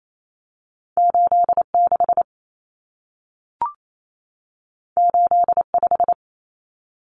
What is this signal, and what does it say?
Morse code, heard as a keyed tone, about 28 wpm, 700 Hz. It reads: 86 85
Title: Numbers - 2-Digit 28wpm